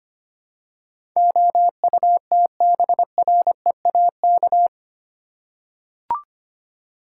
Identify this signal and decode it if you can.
Morse code, heard as a keyed tone, about 25 wpm, 700 Hz. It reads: OUTBREAK
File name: 8 Letter Words - First 3 Letters Given - 25wpm